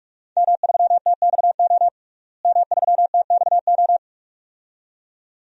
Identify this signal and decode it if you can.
Morse code, heard as a keyed tone, about 43 wpm, 700 Hz. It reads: M3TXO M3TXO
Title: DXCC-100 Call Signs - 5 Characters - Repeated 2x - 45wpm